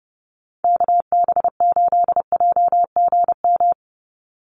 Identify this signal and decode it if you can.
Morse code, heard as a keyed tone, about 30 wpm, 700 Hz. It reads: KB8JGM
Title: USA Call Signs 30wpm